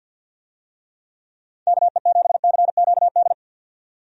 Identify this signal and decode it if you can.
Morse code, heard as a keyed tone, about 51 wpm, 700 Hz. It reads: KE7CQD
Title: USA Call Signs - 6 Characters - 50wpm